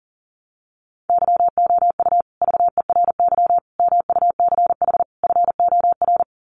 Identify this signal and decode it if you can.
Morse code, heard as a keyed tone, about 40 wpm, 700 Hz. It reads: YOU VERY MUCH FOR
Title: Sets of 4 Words 40wpm